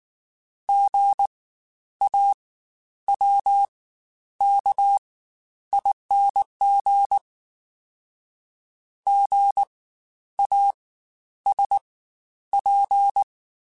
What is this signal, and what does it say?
Morse code, heard as a keyed tone, about 19 wpm, spaced as for 10 wpm, 800 Hz. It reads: GAWKING GASP
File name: Spaced Alphas / Spaced Alphas mp3s / Gs sp @18